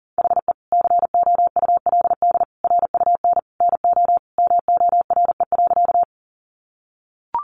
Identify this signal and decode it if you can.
Morse code, heard as a keyed tone, about 40 wpm, 700 Hz. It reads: HE COULD RUN NO MORE.
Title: Sentences from Top 300 Words 40wpm